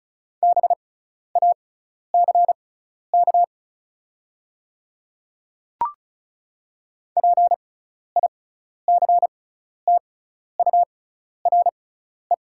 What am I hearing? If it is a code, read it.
Morse code, heard as a keyed tone, about 36 wpm, spaced as for 12 wpm, 690 Hz. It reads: BACK PICTURE
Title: 6x Character Spacing - Top 300 Words - 35wpm